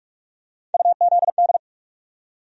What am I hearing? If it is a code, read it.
Morse code, heard as a keyed tone, about 47 wpm, 700 Hz. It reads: UZD
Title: USA Call Sign Postfixes - 3 Letters 45wpm